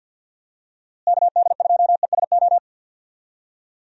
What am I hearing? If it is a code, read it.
Morse code, heard as a keyed tone, about 46 wpm, 700 Hz. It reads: KD2ESO
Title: USA Call Signs - 6 Characters - Repeated 3x - 50wpm